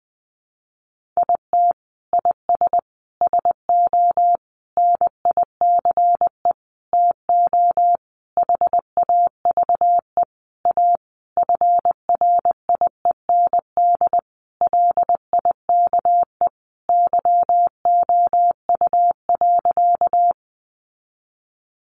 Morse code, heard as a keyed tone, about 20 wpm, 700 Hz. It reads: IT IS SO NICE TO HAVE A FRIEND LIKE YOU.
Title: Sentences from Top 1100 Words 20wpm